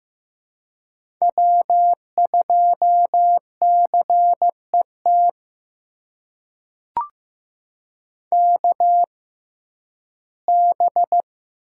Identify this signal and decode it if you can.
Morse code, heard as a keyed tone, about 15 wpm, spaced as for 11 wpm, 700 Hz. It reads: W2CET KB
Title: USA Call Signs - 6x Character Spacing - 15wpm